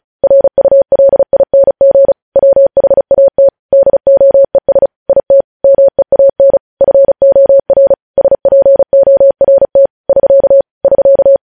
Morse code, heard as a keyed tone, about 36 wpm, 550 Hz. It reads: RULING WHAT DOES IT MEAN FOR SPORT <SK> <SK>
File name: podcast / News headlines-1x-35x35